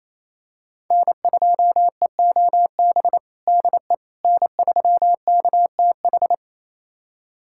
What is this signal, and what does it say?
Morse code, heard as a keyed tone, about 28 wpm, 700 Hz. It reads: N2EOB DE N3KTH
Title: USA Callsign DE Callsign - 5 Characters - 28wpm